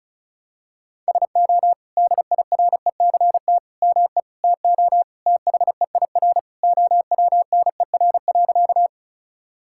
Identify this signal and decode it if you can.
Morse code, heard as a keyed tone, about 34 wpm, 700 Hz. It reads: SO DIRECT ME TO THEIR OWNER.
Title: Sentences from Top 1600 Words 35wpm